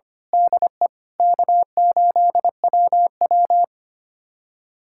Morse code, heard as a keyed tone, about 25 wpm, 700 Hz. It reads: DE K8WW
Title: USA Callsign DE Callsign - 4 Characters - 25wpm